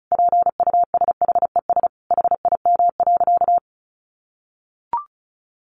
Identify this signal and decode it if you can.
Morse code, heard as a keyed tone, about 34 wpm, 700 Hz. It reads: PUSHES HIM.
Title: Additional Easy Phrases 35wpm